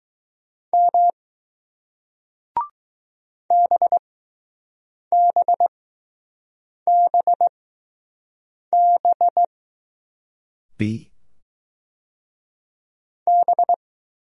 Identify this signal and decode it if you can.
Morse code, heard as a keyed tone, about 20 wpm, spaced as for 6 wpm, 700 Hz. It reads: M BBBB B